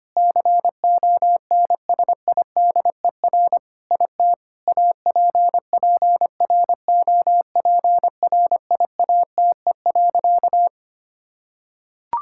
Morse code, heard as a keyed tone, about 25 wpm, 700 Hz. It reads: CONSIDER IT APPROPRIATE.
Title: Sentences from Top 1000 Words - Review Part 5 - 25wpm